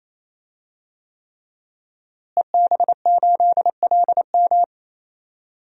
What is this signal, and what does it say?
Morse code, heard as a keyed tone, about 28 wpm, 700 Hz. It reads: EB8LM
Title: DXCC-100 Call Signs - 5 Characters - Repeated 3x - 28wpm